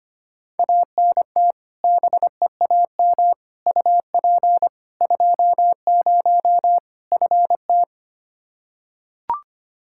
Morse code, heard as a keyed tone, about 25 wpm, 700 Hz. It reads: ANT BEAM UP 20 FT
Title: K1USN SST QSO Practice 25wpm